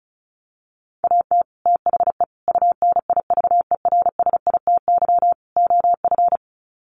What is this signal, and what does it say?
Morse code, heard as a keyed tone, about 34 wpm, 700 Hz. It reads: AT THE UNIVERSITY OF
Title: Sets of 4 Words 35wpm